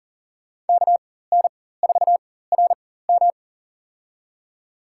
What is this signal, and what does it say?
Morse code, heard as a keyed tone, about 40 wpm, spaced as for 19 wpm, 700 Hz. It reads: KN4RM
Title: USA Call Signs - 5 Characters - 4x Character Spacing - 40wpm